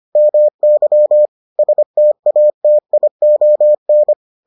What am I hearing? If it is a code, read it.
Morse code, heard as a keyed tone, about 25 wpm, 600 Hz. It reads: MY STATION